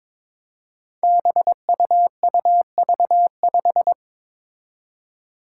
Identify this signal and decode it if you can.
Morse code, heard as a keyed tone, about 22 wpm, 700 Hz. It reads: BUUV5
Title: Five Challenging Letter-Number Combinations 22wpm